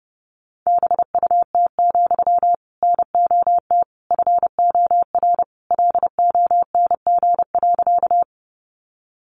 Morse code, heard as a keyed tone, about 30 wpm, 700 Hz. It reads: BUT, NOT FOR LONG.
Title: Sentences from Top 600 Words 30wpm